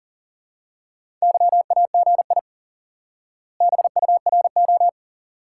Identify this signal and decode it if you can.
Morse code, heard as a keyed tone, about 40 wpm, 700 Hz. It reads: YAGI BURO
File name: IQE Course 3 - Lesson 89 - YAGI - Two Elements - 40wpm